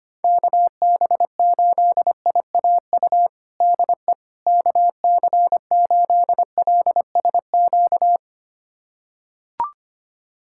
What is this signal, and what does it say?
Morse code, heard as a keyed tone, about 25 wpm, 700 Hz. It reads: KB8IAU DE KC8LSQ
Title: USA Callsign DE Callsign - 6 Characters - 25wpm